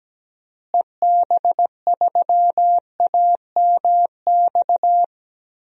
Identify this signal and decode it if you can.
Morse code, heard as a keyed tone, about 17 wpm, 700 Hz. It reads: EB3AMX
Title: DXCC-100 Call Signs - 6 Characters - Repeated 3x - 17wpm